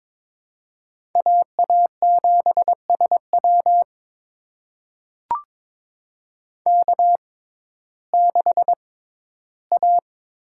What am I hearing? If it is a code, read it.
Morse code, heard as a keyed tone, about 21 wpm, spaced as for 13 wpm, 700 Hz. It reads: AA7SW K6A
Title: USA Call Signs - 5 Characters - 6x Character Spacing - 22wpm